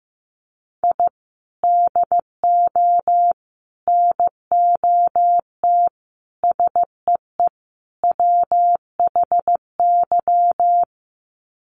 Morse code, heard as a keyed tone, about 14 wpm, 700 Hz. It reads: I DO NOT SEE WHY
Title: Sets of 5 Words Encore 15wpm